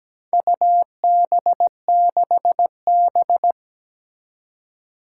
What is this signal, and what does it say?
Morse code, heard as a keyed tone, about 17 wpm, 700 Hz. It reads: UB6B
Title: Four Challenging Letter-Number Combinations 17wpm